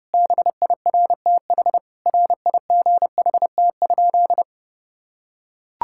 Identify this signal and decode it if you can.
Morse code, heard as a keyed tone, about 32 wpm, 700 Hz. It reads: BIRTH RIGHT?
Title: Sentences from Top 1000 Words - Review Part 5 - 30wpm